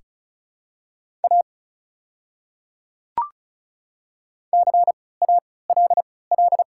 Morse code, heard as a keyed tone, about 36 wpm, spaced as for 20 wpm, 700 Hz. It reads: A CALL
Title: 3x Character Spacing - Top 200 Words - 35wpm